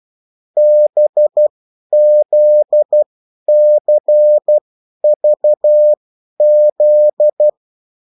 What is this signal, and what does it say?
Morse code, heard as a keyed tone, about 12 wpm, spaced as for 10 wpm, 600 Hz. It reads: BZCVZ
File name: morse mp3 / revision9-vitesse 10 mots